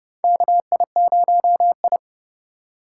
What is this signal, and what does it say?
Morse code, heard as a keyed tone, about 32 wpm, 700 Hz. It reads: KI0I
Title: K1USN SST QSO Practice - 30wpm